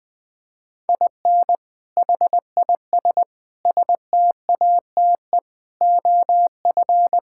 Morse code, heard as a keyed tone, about 20 wpm, 700 Hz. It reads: IN HIS STATE OF
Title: Sentences with 7-Word Phrases - Spoken Beginning - 20wpm